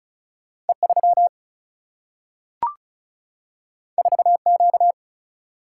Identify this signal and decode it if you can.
Morse code, heard as a keyed tone, about 36 wpm, 700 Hz. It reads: E 3 4 Q
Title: Single-Letter Single-Number 35wpm